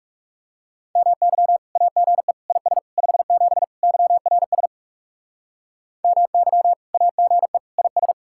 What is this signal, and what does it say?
Morse code, heard as a keyed tone, about 39 wpm, 700 Hz. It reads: MY AGE IS 57 YRS MY AGE IS